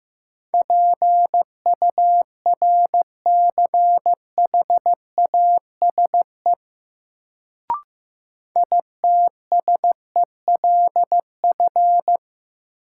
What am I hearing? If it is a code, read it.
Morse code, heard as a keyed tone, about 15 wpm, 700 Hz. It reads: PURCHASE ITSELF
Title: Sentences from Top 800 Words - New Word Drill - 15wpm